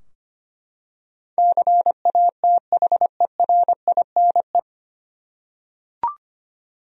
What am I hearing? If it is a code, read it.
Morse code, heard as a keyed tone, about 25 wpm, 700 Hz. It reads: CATHERINE
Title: USA Common Names - 25wpm